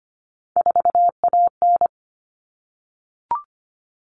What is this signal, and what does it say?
Morse code, heard as a keyed tone, about 25 wpm, 700 Hz. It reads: 4AN
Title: All Cut Numbers - 3-Digit 25wpm